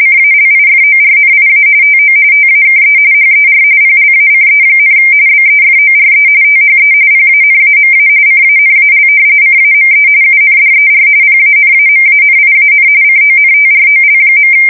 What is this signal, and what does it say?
RTTY, about 45.45 baud. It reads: CALIFORNIA TO TAKE HER PLACE, BEFORE THE PANAMA CANAL CAME UNDER PANAMANIAN CONTROL.
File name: fsk170 / battleshipnj